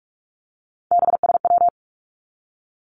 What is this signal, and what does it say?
Morse code, heard as a keyed tone, about 46 wpm, 700 Hz. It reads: BSW
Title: Three Challenging Letter-Number Combinations 45wpm